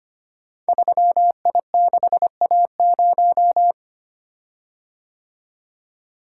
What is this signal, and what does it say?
Morse code, heard as a keyed tone, about 25 wpm, 700 Hz. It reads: 3I6A0
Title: Code Groups - 5 Letter:Numbers - 25wpm